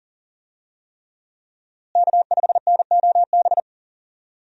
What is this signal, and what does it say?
Morse code, heard as a keyed tone, about 38 wpm, 700 Hz. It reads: K5NOB
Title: USA Call Signs - 5 Characters - Repeated 3x - 40wpm